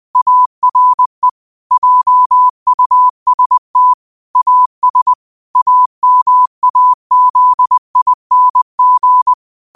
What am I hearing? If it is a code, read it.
Morse code, heard as a keyed tone, about 20 wpm, 1000 Hz. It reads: ARE JUST AS AMAZING
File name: qotdfiles / 20wpm